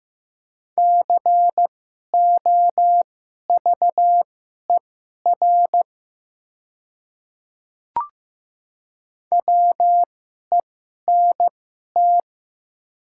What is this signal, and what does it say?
Morse code, heard as a keyed tone, about 15 wpm, spaced as for 11 wpm, 700 Hz. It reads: COVER WENT
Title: 2x Character Spacing - Top 300 Words - 15wpm